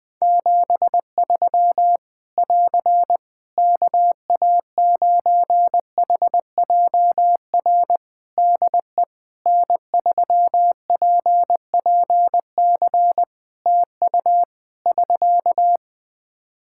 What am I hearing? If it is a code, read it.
Morse code, heard as a keyed tone, about 20 wpm, 700 Hz. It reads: 73 <AR> KA9HJR DE N3PPC TU <SK>
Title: Practice QSO - Exchange 4 - Ending - 20wpm